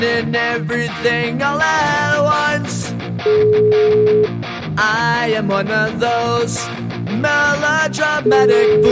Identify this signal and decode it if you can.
Ringback tone (Europe, 425 Hz), with noise.